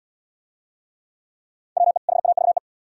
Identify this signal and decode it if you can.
Morse code, heard as a keyed tone, about 73 wpm, 700 Hz. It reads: WE HAVE